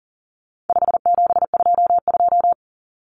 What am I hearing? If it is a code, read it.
Morse code, heard as a keyed tone, about 40 wpm, 700 Hz. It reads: 5722
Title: Common Cut Numbers - 4-Digit 40wpm